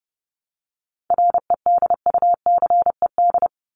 Morse code, heard as a keyed tone, about 29 wpm, 700 Hz. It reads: REDUCED
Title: ED Suffix - Top 100 Words with Base Words 30wpm